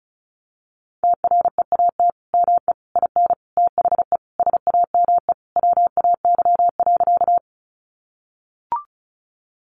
Morse code, heard as a keyed tone, about 34 wpm, 700 Hz. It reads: TREAT ME IN THE SAME WAY.
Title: Sentences from Top 1300 Words 35wpm